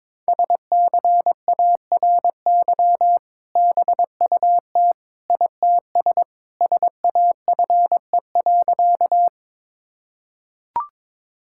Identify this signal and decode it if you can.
Morse code, heard as a keyed tone, about 22 wpm, 700 Hz. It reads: SCARY BUT ITS SAFE.
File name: practice / 22wpm / Sentences with 2-Word Phrases - 22wpm